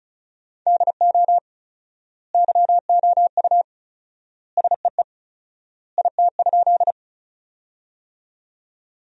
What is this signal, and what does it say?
Morse code, heard as a keyed tone, about 36 wpm, 700 Hz. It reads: DO YOU SEE IT?
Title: 4x Word-Spacing - Sentences from Top 100 Words - 35wpm